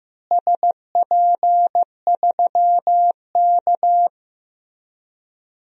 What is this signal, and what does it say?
Morse code, heard as a keyed tone, about 15 wpm, 700 Hz. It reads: SP3K
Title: DXCC-100 Call Signs - 4 Characters - Repeated 3x - 15wpm